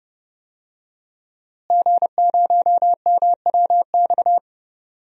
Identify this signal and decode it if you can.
Morse code, heard as a keyed tone, about 30 wpm, 700 Hz. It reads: G0MWX
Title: DXCC-100 Call Signs - 5 Characters - Repeated 2x - 30wpm